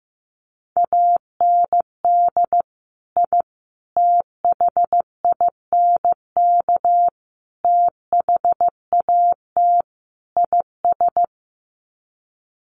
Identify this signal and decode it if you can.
Morse code, heard as a keyed tone, about 15 wpm, 700 Hz. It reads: AND I THINK THAT IS
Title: Sets of 5 Words Encore2 15wpm